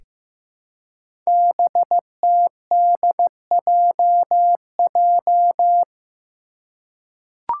Morse code, heard as a keyed tone, about 15 wpm, 700 Hz. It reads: BTDJJ
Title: Five Challenging Letter-Number Combinations 15wpm